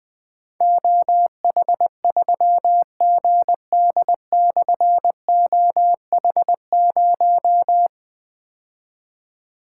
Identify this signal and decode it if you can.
Morse code, heard as a keyed tone, about 20 wpm, 700 Hz. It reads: OH3GD/OH0
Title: Difficult Call Signs 20wpm